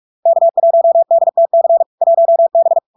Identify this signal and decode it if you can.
Morse code, heard as a keyed tone, about 46 wpm, 670 Hz. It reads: K1DTC 1B
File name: LARK-FD-2022-Log-in-CW-45wpm-670-Hz